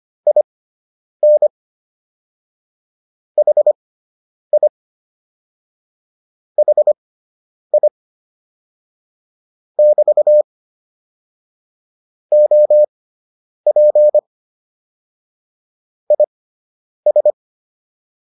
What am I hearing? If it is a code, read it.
Morse code, heard as a keyed tone, about 25 wpm, spaced as for 9 wpm, 600 Hz. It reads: IN HI HI <BT> OP IS